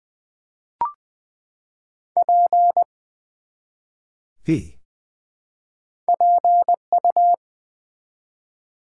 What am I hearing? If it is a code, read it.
Morse code, heard as a keyed tone, about 20 wpm, 700 Hz. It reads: P PU
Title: Build A Word - 1 Letter At A Time - 4-Letter Words - 20wpm